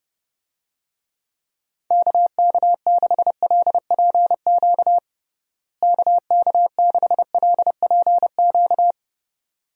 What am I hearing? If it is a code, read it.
Morse code, heard as a keyed tone, about 30 wpm, 700 Hz. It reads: KK6LPQ KK6LPQ
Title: USA Call Signs - 6 Characters - Repeated 2x - 30wpm